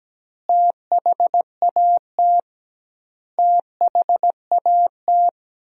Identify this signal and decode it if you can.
Morse code, heard as a keyed tone, about 17 wpm, 700 Hz. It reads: THAT THAT
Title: 2x Word-Spacing - Sentences from Top 200 Words - 17wpm